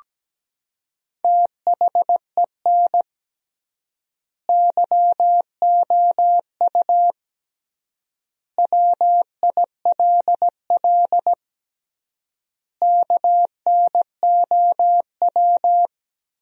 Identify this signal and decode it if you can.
Morse code, heard as a keyed tone, about 17 wpm, 700 Hz. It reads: THEN YOU WILL KNOW
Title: 3x Word-Spacing - Sentences from Top 400 Words - 17wpm